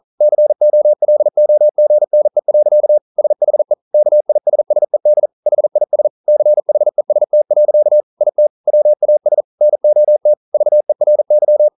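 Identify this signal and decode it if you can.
Morse code, heard as a keyed tone, about 40 wpm, 600 Hz. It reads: COLOGNE. SHE KISSED HIS CHEST. IT WAS NOT VERY
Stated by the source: Morse code practice at 41 wpm, from CW med Gnister.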